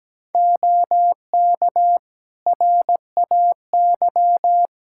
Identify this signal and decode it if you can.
Morse code, heard as a keyed tone, about 17 wpm, 700 Hz. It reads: OK RAY
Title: Practice QSO - Exchange 3 - More Info - 17wpm